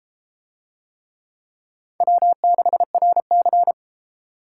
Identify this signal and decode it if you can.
Morse code, heard as a keyed tone, about 32 wpm, 700 Hz. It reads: W6RC